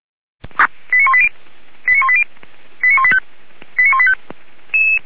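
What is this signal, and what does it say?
Five-tone selective call (ZVEI), heard as quick runs of five tones, tones of about 70 ms. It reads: 89180 89180 89185 89185
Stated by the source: ZVEI five-tone call received on an SDR, logged by Feuerwehr Orlamünde.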